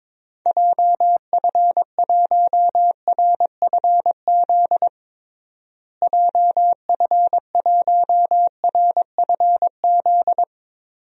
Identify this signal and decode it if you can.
Morse code, heard as a keyed tone, about 21 wpm, 700 Hz. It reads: JF1RFZ JF1RFZ
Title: DXCC-100 Call Signs - 6 Characters - Repeated 2x - 22wpm